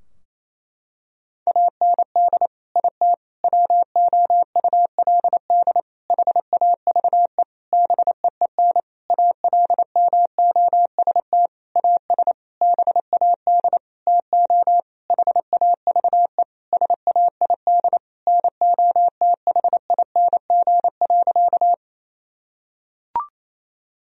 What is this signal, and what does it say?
Morse code, heard as a keyed tone, about 28 wpm, 700 Hz. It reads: AND IT WOULD HAVE BEEN ALMOST AS BAD TO HAVE SAID NOTHING.
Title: Sentences from Top 1000 Words - Review Part 4 - 28wpm